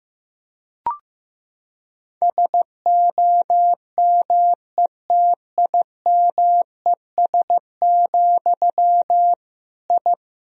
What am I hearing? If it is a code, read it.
Morse code, heard as a keyed tone, about 15 wpm, 700 Hz. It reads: SOMETIMES, I
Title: Sentences from Top 1000 Words - Review Part 3 - 15wpm